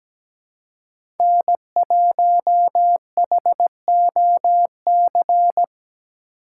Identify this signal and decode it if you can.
Morse code, heard as a keyed tone, about 17 wpm, 700 Hz. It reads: N1HOC
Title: USA Call Signs - 5 Characters - 6x Character Spacing - 17wpm